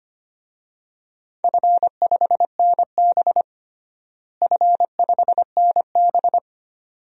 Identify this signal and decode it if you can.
Morse code, heard as a keyed tone, about 25 wpm, 700 Hz. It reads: F5NB F5NB
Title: DXCC-100 Call Signs - 4 Characters - Repeated 2x - 25wpm